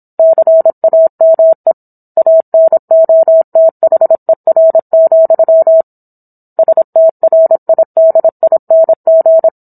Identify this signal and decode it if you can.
Morse code, heard as a keyed tone, about 26 wpm, 650 Hz. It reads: CAME ANOTHER, STRIDING